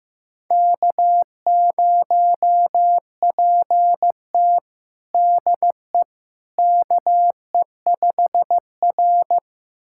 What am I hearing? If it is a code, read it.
Morse code, heard as a keyed tone, about 15 wpm, 700 Hz. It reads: K0PT DE KE5R
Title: USA Callsign DE Callsign - 4 Characters - 15wpm